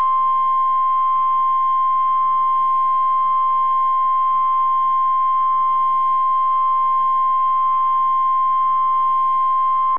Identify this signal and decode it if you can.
Emergency Alert System (SAME) tones (the 1050 Hz attention tone). It is heard from 0.6 to 9.8 s.